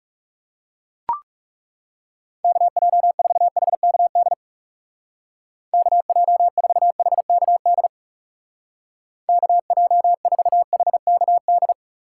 Morse code, heard as a keyed tone, about 38 wpm, 700 Hz. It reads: KJ4HKD KJ4HKD KJ4HKD
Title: Speed Racing - USA 2x3 Call Signs 30wpm